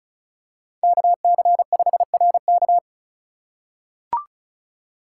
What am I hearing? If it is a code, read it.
Morse code, heard as a keyed tone, about 36 wpm, 700 Hz. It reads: KC5RK
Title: USA Call Signs - 4x Character Spacing - 35wpm